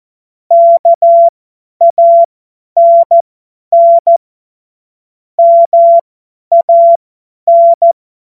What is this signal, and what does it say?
Morse code, heard as a keyed tone, about 14 wpm, spaced as for 10 wpm, 680 Hz. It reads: KANN MAN
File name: Morsekurs / Witze / Witz 28. September 2021, 50 F70